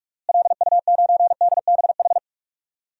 Morse code, heard as a keyed tone, about 46 wpm, 700 Hz. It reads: LU9DBH
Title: DXCC-100 Call Signs - 6 Characters - Repeated 2x - 45wpm